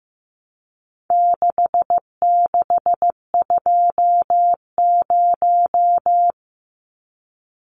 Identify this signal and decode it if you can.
Morse code, heard as a keyed tone, about 15 wpm, 700 Hz. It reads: 6620
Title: Numbers - 4-Digit 15wpm